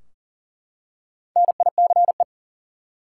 Morse code, heard as a keyed tone, about 38 wpm, 700 Hz. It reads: NICE
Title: Four Letter Words - 40wpm